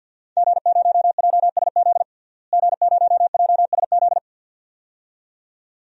Morse code, heard as a keyed tone, about 52 wpm, 700 Hz. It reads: G0JSZ G0JSZ
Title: DXCC-100 Call Signs - 5 Characters - Repeated 2x - 50wpm